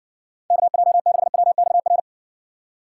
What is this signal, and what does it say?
Morse code, heard as a keyed tone, about 59 wpm, 700 Hz. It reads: KJ6WXR